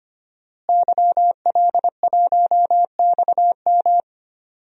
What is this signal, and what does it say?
Morse code, heard as a keyed tone, about 25 wpm, 700 Hz. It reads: YL1XM
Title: DXCC-100 Call Signs - 5 Characters - Repeated 2x - 25wpm